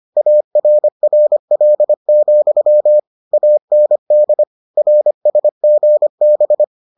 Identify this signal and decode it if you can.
Morse code, heard as a keyed tone, about 25 wpm, 600 Hz. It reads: ARRL, AND RSGB